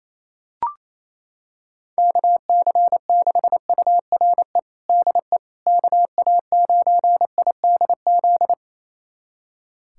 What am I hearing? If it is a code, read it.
Morse code, heard as a keyed tone, about 28 wpm, 700 Hz. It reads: KC6URE DE KA9IDZ
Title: USA Callsign DE Callsign - 6 Characters - 28wpm